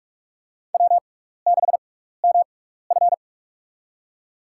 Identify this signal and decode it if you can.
Morse code, heard as a keyed tone, about 46 wpm, spaced as for 15 wpm, 700 Hz. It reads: W6MF
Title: USA Call Signs - 6x Character Spacing - 45wpm